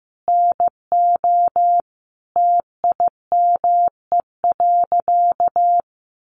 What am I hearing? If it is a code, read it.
Morse code, heard as a keyed tone, about 15 wpm, 700 Hz. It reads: NO TIME.
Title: Sentences from Top 100 Words 15wpm